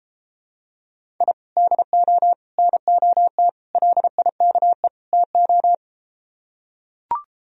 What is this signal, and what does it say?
Morse code, heard as a keyed tone, about 33 wpm, 700 Hz. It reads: I DO NOT LIKE TO